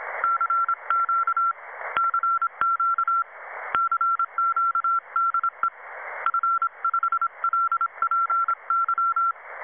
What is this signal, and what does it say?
Morse code, heard as a keyed tone, about 26 wpm, 1340 Hz. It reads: CQ CQ CQ DE F5LPY
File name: Websdr Recording 2014 07 21t07 22 13z 7011 4khz